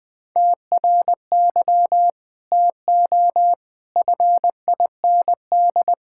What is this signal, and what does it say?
Morse code, heard as a keyed tone, about 21 wpm, 700 Hz. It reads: TRY TO FIND
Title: Sentences with 2-Word Phrases - 20wpm